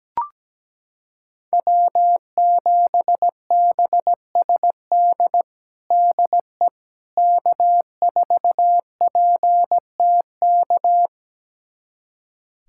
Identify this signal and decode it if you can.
Morse code, heard as a keyed tone, about 17 wpm, 700 Hz. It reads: W7BSD DE K4PTK
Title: USA Callsign DE Callsign - 5 Characters - 17wpm